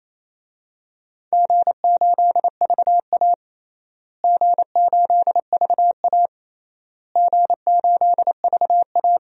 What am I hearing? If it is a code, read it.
Morse code, heard as a keyed tone, about 28 wpm, 700 Hz. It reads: G8VA G8VA G8VA
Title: DXCC-100 Call Signs - 4 Characters - Repeated 3x - 28wpm